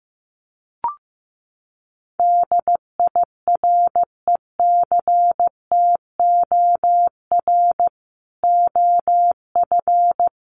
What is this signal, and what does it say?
Morse code, heard as a keyed tone, about 15 wpm, 700 Hz. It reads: DIRECTOR OF
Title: Sets of 3 Words 15wpm